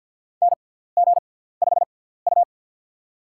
Morse code, heard as a keyed tone, about 47 wpm, spaced as for 17 wpm, 700 Hz. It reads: NG5U
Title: USA Call Signs - 6x Character Spacing - 50wpm